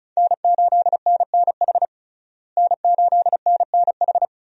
Morse code, heard as a keyed tone, about 34 wpm, 700 Hz. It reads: N8NNH N8NNH
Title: USA Call Signs - 5 Characters - Repeated 2x - 35wpm